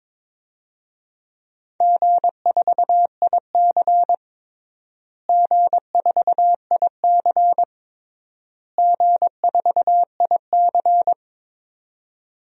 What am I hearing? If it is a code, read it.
Morse code, heard as a keyed tone, about 22 wpm, 700 Hz. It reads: G4IC G4IC G4IC
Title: DXCC-100 Call Signs - 4 Characters - Repeated 3x - 22wpm